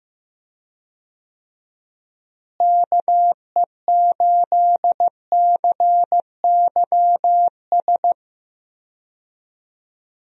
Morse code, heard as a keyed tone, about 15 wpm, 700 Hz. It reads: KE8CYS
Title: USA Call Signs - 6 Characters - Repeated 2x - 15wpm